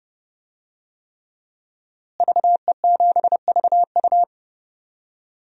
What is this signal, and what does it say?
Morse code, heard as a keyed tone, about 30 wpm, 700 Hz. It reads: VE7VU
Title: DXCC-100 Call Signs - 5 Characters - Repeated 3x - 30wpm